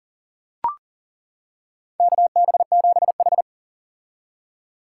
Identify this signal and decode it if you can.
Morse code, heard as a keyed tone, about 40 wpm, 700 Hz. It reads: KB7H
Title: K1USN SST QSO Practice - 40wpm